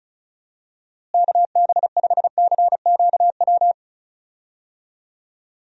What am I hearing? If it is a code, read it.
Morse code, heard as a keyed tone, about 36 wpm, 700 Hz. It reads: KB5CQW
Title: USA Call Signs - 6 Characters - Repeated 2x - 35wpm